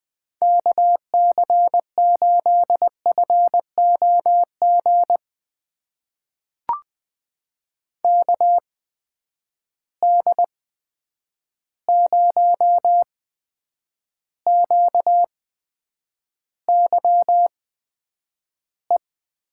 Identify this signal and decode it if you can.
Morse code, heard as a keyed tone, about 20 wpm, spaced as for 9 wpm, 700 Hz. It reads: KC8FOG KD0QYE
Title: USA Call Signs - 6 Characters - 8x Character Spacing - 20wpm